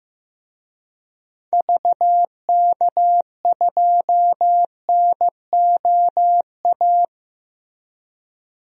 Morse code, heard as a keyed tone, about 15 wpm, 700 Hz. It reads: VK2NOA
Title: DXCC-100 Call Signs - 6 Characters - Repeated 2x - 15wpm